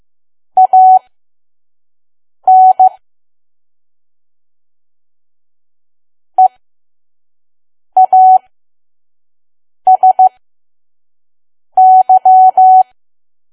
Morse code, heard as a keyed tone, about 15 wpm, spaced as for 5 wpm, 750 Hz. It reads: AN EASY
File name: Archive / 5 WPM files / PM